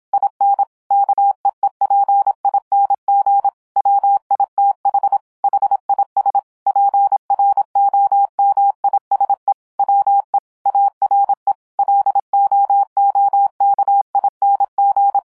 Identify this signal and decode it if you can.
Morse code, heard as a keyed tone, about 26 wpm, 800 Hz. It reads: IN KEEPING WITH HIS PROMISE WE ARE LOOKING